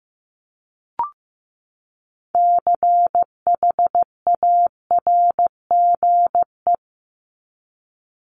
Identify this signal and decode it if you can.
Morse code, heard as a keyed tone, about 15 wpm, 700 Hz. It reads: CHARGE
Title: Top 751-1000 Words 15wpm